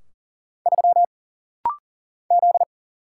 Morse code, heard as a keyed tone, about 40 wpm, 700 Hz. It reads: 3 Z
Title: Single Letter-Number Alt - Rapid-Fire 40wpm